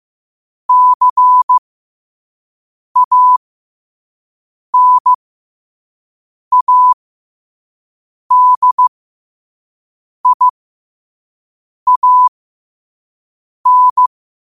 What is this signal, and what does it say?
Morse code, heard as a keyed tone, about 15 wpm, spaced as for 5 wpm, 1000 Hz. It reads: CANADIAN
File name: qotdfiles / 5wpm